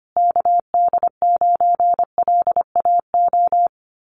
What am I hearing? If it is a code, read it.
Morse code, heard as a keyed tone, about 25 wpm, 700 Hz. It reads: KD9LAO
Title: Common QSO Phrases Encore 25wpm